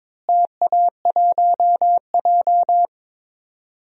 Morse code, heard as a keyed tone, about 22 wpm, 700 Hz. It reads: TA1J
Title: DXCC-100 Call Signs - 4 Characters - Repeated 3x - 22wpm